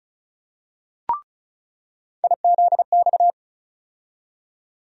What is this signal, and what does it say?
Morse code, heard as a keyed tone, about 36 wpm, 700 Hz. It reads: IZX
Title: USA Call Sign Postfixes - 3 Letters 35wpm